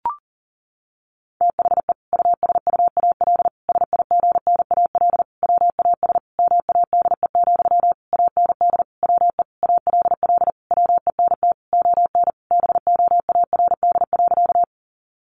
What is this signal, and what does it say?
Morse code, heard as a keyed tone, about 37 wpm, 700 Hz. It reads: THE USUAL SIGNAL WAS MADE, AND WE ALL WENT ON BOARD.
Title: Sentences from Top 1400 Words 40wpm